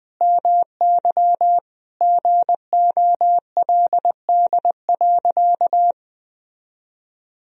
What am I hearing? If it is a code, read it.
Morse code, heard as a keyed tone, about 20 wpm, 700 Hz. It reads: MY GOLD.
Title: Sentences from Top 1000 Words - Review Part 3 - 20wpm